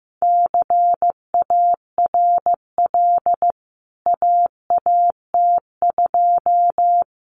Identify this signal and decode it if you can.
Morse code, heard as a keyed tone, about 15 wpm, 700 Hz. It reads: CARL AAT2
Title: CWT Exchange 15wpm